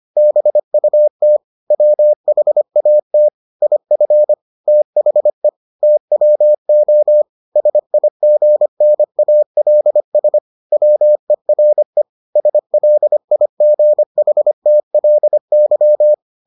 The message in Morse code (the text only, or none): BUT WHAT IF THE TWO SIGNALS WERE SLIGHTLY